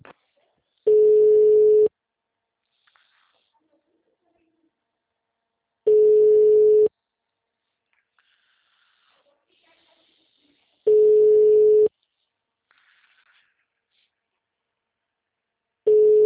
Ringback tone (Europe, 425 Hz).